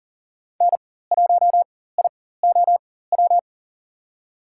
Morse code, heard as a keyed tone, about 40 wpm, spaced as for 19 wpm, 700 Hz. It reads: N1IOW
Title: USA Call Signs - 5 Characters - 4x Character Spacing - 40wpm